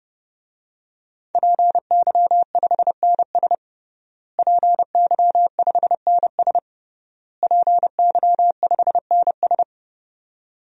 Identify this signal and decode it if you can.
Morse code, heard as a keyed tone, about 32 wpm, 700 Hz. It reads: PY5NS PY5NS PY5NS
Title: DXCC-100 Call Signs - 5 Characters - Repeated 3x - 30wpm